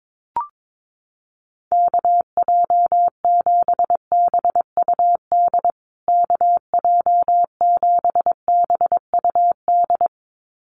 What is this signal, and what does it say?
Morse code, heard as a keyed tone, about 22 wpm, 700 Hz. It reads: KJ7BUD KJ7BUD
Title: USA Call Signs Repeated 2x 22wpm